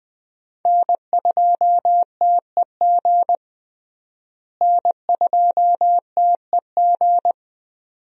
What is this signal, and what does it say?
Morse code, heard as a keyed tone, about 21 wpm, 700 Hz. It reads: N2TEG N2TEG
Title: USA Call Signs - 5 Characters - Repeated 2x - 20wpm